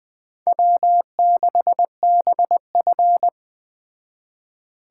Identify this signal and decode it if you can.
Morse code, heard as a keyed tone, about 20 wpm, 700 Hz. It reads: W6BF
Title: Speed Racing - USA 1x2 Call Signs 17wpm